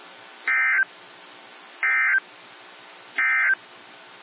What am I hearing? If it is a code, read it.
Emergency Alert System (SAME) tones, heard as harsh buzzing bursts. It is heard from 0.4 to 3.6 s.
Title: NOAA Weather Radio - WXJ73